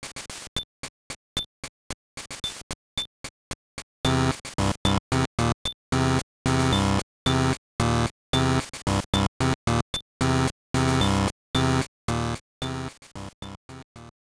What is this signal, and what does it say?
There is no code.